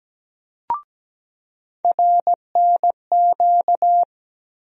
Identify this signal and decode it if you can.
Morse code, heard as a keyed tone, about 17 wpm, 700 Hz. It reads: RNQ
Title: USA Call Sign Postfixes - 3 Letters 17wpm